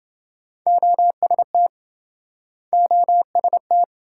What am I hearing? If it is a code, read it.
Morse code, heard as a keyed tone, about 28 wpm, 700 Hz. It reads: OST OST